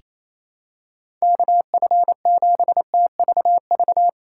Morse code, heard as a keyed tone, about 28 wpm, 700 Hz. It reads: KF7TVV
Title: USA Call Signs - 6 Characters - 8x Character Spacing - 28wpm